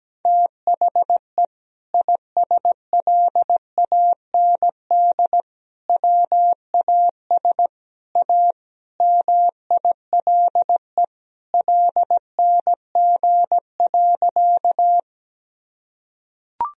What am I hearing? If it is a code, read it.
Morse code, heard as a keyed tone, about 17 wpm, 700 Hz. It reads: THE ISLAND WAS A MILE LNG.
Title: Sentences from Top 500 Words and Abbreviations 17wpm